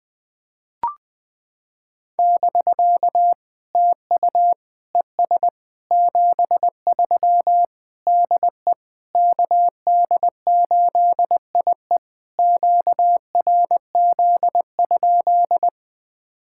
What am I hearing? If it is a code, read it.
Morse code, heard as a keyed tone, about 20 wpm, 700 Hz. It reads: <BK> TU ES 73 DE KD8IE QRZ?
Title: POTA QSO Practice - 20wpm